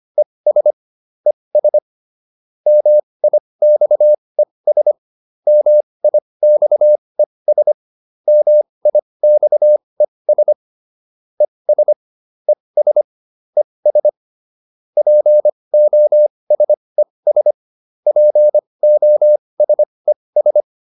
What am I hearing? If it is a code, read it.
Morse code, heard as a keyed tone, about 25 wpm, spaced as for 20 wpm, 600 Hz. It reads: ES ES MIXES MIXES MIXES ES ES ES POSES POSES